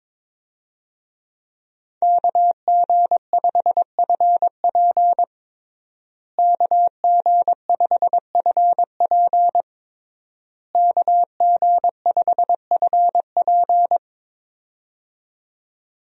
Morse code, heard as a keyed tone, about 22 wpm, 700 Hz. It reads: KG5FP KG5FP KG5FP
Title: USA Call Signs - 5 Characters - Repeated 3x - 22wpm